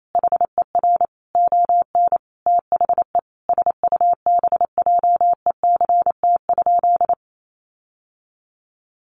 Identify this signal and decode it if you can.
Morse code, heard as a keyed tone, about 28 wpm, 700 Hz. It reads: HER ON THE SUBJECT?
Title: Sentences from Top 800 Words 28wpm